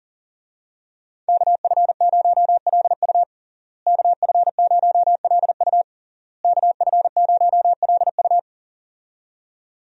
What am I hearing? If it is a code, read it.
Morse code, heard as a keyed tone, about 40 wpm, 700 Hz. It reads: KF0LU KF0LU KF0LU
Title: USA Call Signs - 5 Characters - Repeated 3x - 40wpm